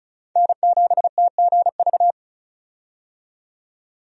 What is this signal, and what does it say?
Morse code, heard as a keyed tone, about 34 wpm, 700 Hz. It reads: N7TGV
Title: USA Call Signs - 5 Characters - 35wpm